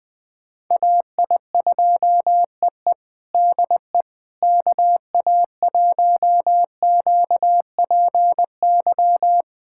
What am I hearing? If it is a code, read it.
Morse code, heard as a keyed tone, about 21 wpm, 700 Hz. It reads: AI2EE DE KA1QPY
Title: USA Callsign DE Callsign 20wpm